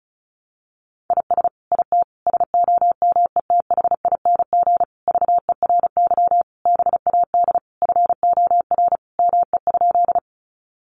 Morse code, heard as a keyed tone, about 34 wpm, 700 Hz. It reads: IS IT SOMETHING VERY BAD FOR ME?
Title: Sentences from Top 700 Words 35wpm